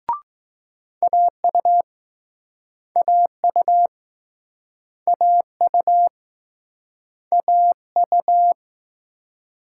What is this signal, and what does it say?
Morse code, heard as a keyed tone, about 21 wpm, 700 Hz. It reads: AU AU AU AU